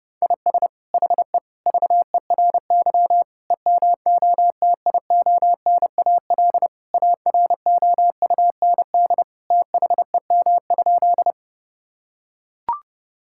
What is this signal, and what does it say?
Morse code, heard as a keyed tone, about 30 wpm, 700 Hz. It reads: IS HE VERY EMOTIONAL AROUND THEM?
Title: Sentences from Top 1000 Words - Review Part 5 - 30wpm